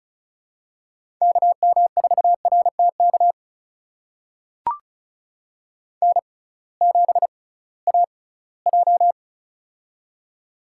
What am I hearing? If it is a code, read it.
Morse code, heard as a keyed tone, about 36 wpm, spaced as for 21 wpm, 700 Hz. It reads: KM4RTK N7AJ
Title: USA Call Signs - 6x Character Spacing - 35wpm